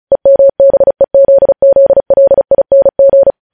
Morse code, heard as a keyed tone, about 36 wpm, 550 Hz. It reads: EMBEZZLING